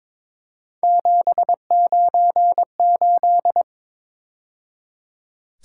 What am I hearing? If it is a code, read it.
Morse code, heard as a keyed tone, about 26 wpm, 700 Hz. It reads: 798
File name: Speed-Racing - Three Letter-Number Combinations - 22wpm